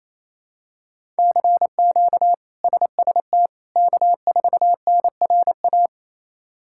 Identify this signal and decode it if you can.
Morse code, heard as a keyed tone, about 28 wpm, 700 Hz. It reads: CQ SST K4NRA
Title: K1USN SST QSO Practice - 28wpm